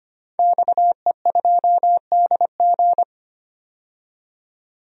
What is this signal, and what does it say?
Morse code, heard as a keyed tone, about 25 wpm, 690 Hz. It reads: XE2DG
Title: DXCC-100 Call Signs - 5 Characters - 25wpm